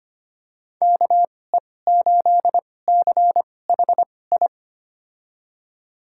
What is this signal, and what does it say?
Morse code, heard as a keyed tone, about 25 wpm, spaced as for 18 wpm, 700 Hz. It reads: KE8CHI
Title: USA Call Signs - 6 Characters - 2x Character Spacing - 25wpm